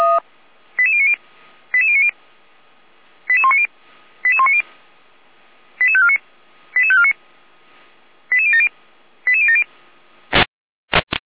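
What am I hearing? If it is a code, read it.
Five-tone selective call (ZVEI), heard as quick runs of five tones, tones of about 70 ms. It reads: 88090 88090 88190 88190 88549 88549 88070 88070
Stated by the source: ZVEI five-tone call received on SDR.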